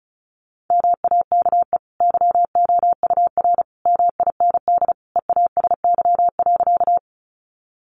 Morse code, heard as a keyed tone, about 34 wpm, 700 Hz. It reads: MAKE YOUR MIND EASY.
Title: Sentences from Top 700 Words 35wpm